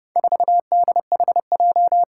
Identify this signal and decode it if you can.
Morse code, heard as a keyed tone, about 30 wpm, 700 Hz. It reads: 4DHJ
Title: Four Challenging Letter-Number Combinations 30wpm